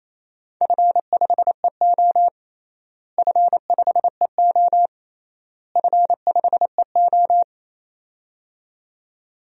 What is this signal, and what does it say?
Morse code, heard as a keyed tone, about 27 wpm, 700 Hz. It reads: F5EO F5EO F5EO
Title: DXCC-100 Call Signs - 4 Characters - Repeated 3x - 28wpm